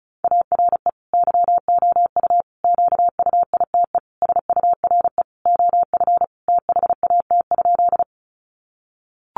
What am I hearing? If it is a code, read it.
Morse code, heard as a keyed tone, about 34 wpm, 700 Hz. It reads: ARE YOU QUITE SURE OF THAT?
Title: Sentences from Top 700 Words 35wpm